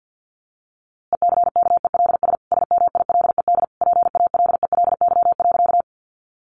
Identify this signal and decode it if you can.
Morse code, heard as a keyed tone, about 50 wpm, 700 Hz. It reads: ECKELS SMILED PALELY.
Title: A Sound Of Thunder 50wpm